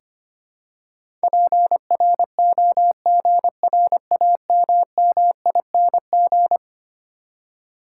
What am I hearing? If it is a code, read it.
Morse code, heard as a keyed tone, about 25 wpm, 700 Hz. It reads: PROGRAMMING
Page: morse-code-ninja-practice-25wpm